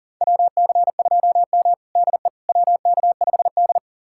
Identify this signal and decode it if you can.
Morse code, heard as a keyed tone, about 39 wpm, 700 Hz. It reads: WC2M DE WK5D
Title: USA Callsign DE Callsign - 4 Characters - 40wpm